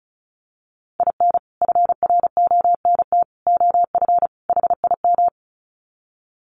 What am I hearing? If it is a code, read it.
Morse code, heard as a keyed tone, about 34 wpm, 700 Hz. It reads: IN FRONT OF HIM
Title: Sets of 4 Words 35wpm